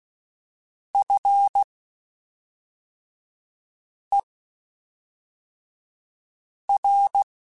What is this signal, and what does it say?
Morse code, heard as a keyed tone, about 16 wpm, spaced as for 3 wpm, 800 Hz. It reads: FER
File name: (spaced) / QSOs Spaced-mp3s / kc7haa @5 sp